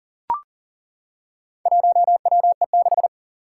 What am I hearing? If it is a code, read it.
Morse code, heard as a keyed tone, about 40 wpm, 700 Hz. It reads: 1WE6
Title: Four Challenging Letter-Number Combinations 40wpm